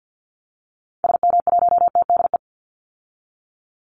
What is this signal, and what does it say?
Morse code, heard as a keyed tone, about 50 wpm, 700 Hz. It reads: SM1TDE
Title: Call Signs Repeated 3x - 50wpm